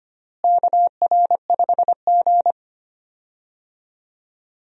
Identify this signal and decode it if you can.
Morse code, heard as a keyed tone, about 25 wpm, 700 Hz. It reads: KR5G
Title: USA Call Signs - 4 Characters - Repeated 2x - 25wpm